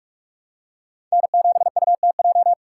Morse code, heard as a keyed tone, about 44 wpm, 700 Hz. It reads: N7UTJ